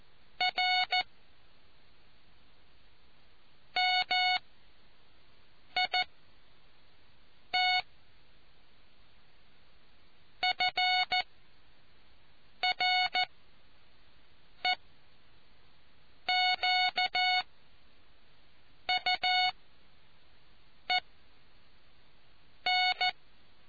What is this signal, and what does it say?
Morse code, heard as a keyed tone, about 14 wpm, spaced as for 5 wpm, 730 Hz. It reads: R MIT FREQUEN